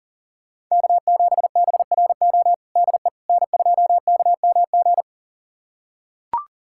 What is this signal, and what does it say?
Morse code, heard as a keyed tone, about 40 wpm, 700 Hz. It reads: K7BRO DE N2KMG
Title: USA Callsign DE Callsign - 5 Characters - 40wpm